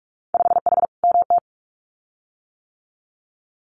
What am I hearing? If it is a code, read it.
Morse code, heard as a keyed tone, about 46 wpm, 690 Hz. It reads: 5H MT
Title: Winter Field Day Exchanges Encore 45wpm